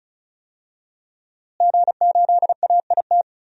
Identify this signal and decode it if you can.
Morse code, heard as a keyed tone, about 36 wpm, 700 Hz. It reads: G8AIT
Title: DXCC-100 Call Signs - 5 Characters - Repeated 3x - 35wpm